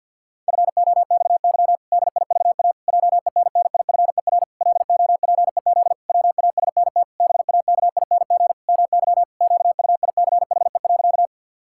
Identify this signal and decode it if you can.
Morse code, heard as a keyed tone, about 50 wpm, 700 Hz. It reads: FOXY DIVA JENNIFER LOPEZ WASNT BAKING MY QUICHE.
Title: Pangram Challenge v2 - 50wpm